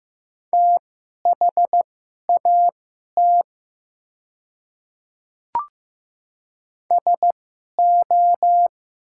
Morse code, heard as a keyed tone, about 15 wpm, spaced as for 11 wpm, 700 Hz. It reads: THAT SO
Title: 2x Character Spacing - Top 100 Words - 15wpm